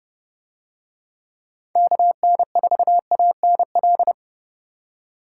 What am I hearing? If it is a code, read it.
Morse code, heard as a keyed tone, about 27 wpm, 700 Hz. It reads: KN4ANL